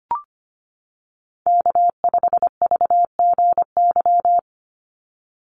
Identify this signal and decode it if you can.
Morse code, heard as a keyed tone, about 25 wpm, 700 Hz. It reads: K5VGY
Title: USA Call Signs 25wpm